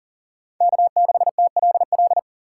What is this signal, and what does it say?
Morse code, heard as a keyed tone, about 49 wpm, 700 Hz. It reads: K6TLL
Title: Speed Racing - USA Call Signs 40wpm